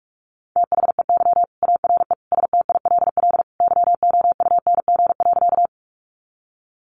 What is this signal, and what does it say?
Morse code, heard as a keyed tone, about 44 wpm, 700 Hz. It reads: THEY ARE STILL YOUNG.
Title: Sentences from Top 400 Words 45wpm